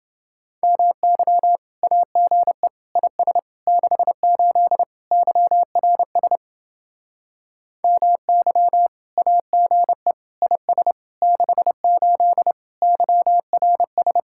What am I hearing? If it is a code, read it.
Morse code, heard as a keyed tone, about 28 wpm, 700 Hz. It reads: MY AGE IS 68 YRS MY AGE IS 68 YRS